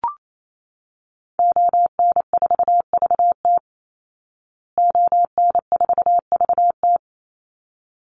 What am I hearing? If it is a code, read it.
Morse code, heard as a keyed tone, about 28 wpm, 700 Hz. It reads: ON4VT ON4VT
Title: Call Signs Repeated 3x - 28wpm